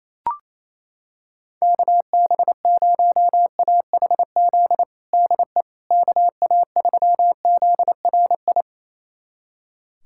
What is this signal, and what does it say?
Morse code, heard as a keyed tone, about 27 wpm, 700 Hz. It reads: KB0AHZ DE KA3ZRI
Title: USA Callsign DE Callsign - 6 Characters - 28wpm